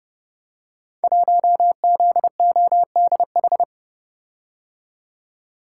Morse code, heard as a keyed tone, about 29 wpm, 700 Hz. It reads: 1ZODH
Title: Code Groups - 5 Letter:Numbers - 30wpm